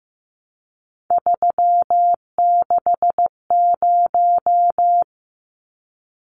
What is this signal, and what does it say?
Morse code, heard as a keyed tone, about 15 wpm, 700 Hz. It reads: 360
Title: Numbers - 3-Digit 15wpm